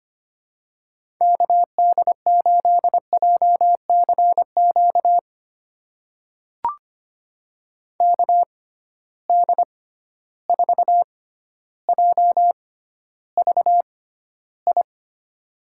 Morse code, heard as a keyed tone, about 25 wpm, spaced as for 13 wpm, 700 Hz. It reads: KD8JCQ KD4JVI
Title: USA Call Signs - 6 Characters - 6x Character Spacing - 25wpm